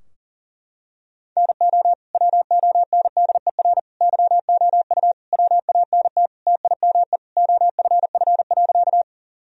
Morse code, heard as a keyed tone, about 40 wpm, 700 Hz. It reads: NO WONDER YOU WANT TIME OFF.
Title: Sentences from Top 1000 Words - Review Part 3 - 40wpm